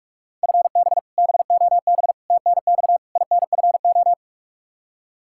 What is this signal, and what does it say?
Morse code, heard as a keyed tone, about 43 wpm, 700 Hz. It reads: FB BOB TNX INFO
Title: K1USN SST QSO Practice 45wpm